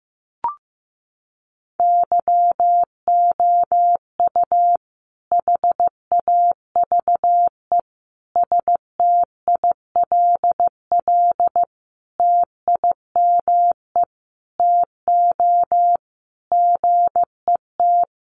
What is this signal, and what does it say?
Morse code, heard as a keyed tone, about 15 wpm, 700 Hz. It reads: YOU HAVE STILL TIME TO GET
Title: Sentences from Top 600 Words 15wpm